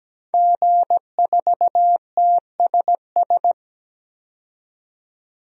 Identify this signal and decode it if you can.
Morse code, heard as a keyed tone, about 17 wpm, 700 Hz. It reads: G4TSS
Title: DXCC-100 Call Signs - 5 Characters - 17wpm